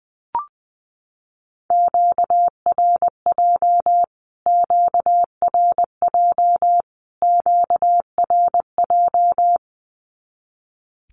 Morse code, heard as a keyed tone, about 20 wpm, 700 Hz. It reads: QRJ QRJ QRJ
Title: Q-Codes Repeated 3x 20wpm